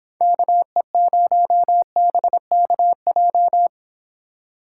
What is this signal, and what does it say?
Morse code, heard as a keyed tone, about 29 wpm, 700 Hz. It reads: KE0BKJ